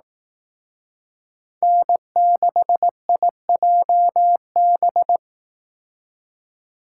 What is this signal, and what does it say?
Morse code, heard as a keyed tone, about 18 wpm, 700 Hz. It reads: N6IJB
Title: Speed Racing - USA 1x3 Call Signs 15wpm